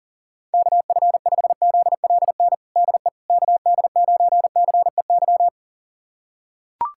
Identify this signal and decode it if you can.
Morse code, heard as a keyed tone, about 40 wpm, 700 Hz. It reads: KF5ZLN DE KD9CEY
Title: USA Callsign DE Callsign 40wpm